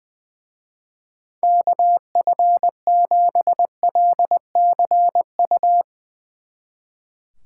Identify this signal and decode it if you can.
Morse code, heard as a keyed tone, about 21 wpm, 700 Hz. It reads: KF7LCU
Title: Speed Racing - USA Call Signs 20wpm